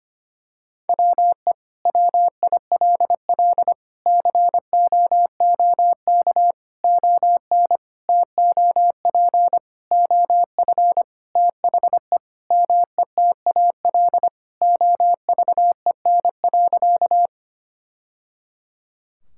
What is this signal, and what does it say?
Morse code, heard as a keyed tone, about 25 wpm, 700 Hz. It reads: WE WILL COOK ON TOP OF THE METAL OVEN.
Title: Sentences from Top 1000 Words - Review Part 6 - 25wpm